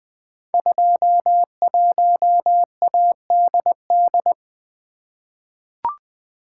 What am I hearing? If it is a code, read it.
Morse code, heard as a keyed tone, about 20 wpm, 700 Hz. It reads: 21ADD
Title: Code Groups - 5 Letter:Numbers - 20wpm